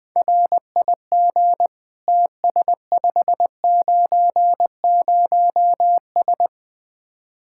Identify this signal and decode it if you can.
Morse code, heard as a keyed tone, about 20 wpm, 700 Hz. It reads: RIG TS590S
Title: Speed-Racing Common QSO Phrases - 17wpm